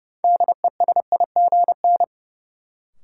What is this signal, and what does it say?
Morse code, heard as a keyed tone, about 31 wpm, 700 Hz. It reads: DESIGN
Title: Sentences from Top 800 Words - New Word Drill - 30wpm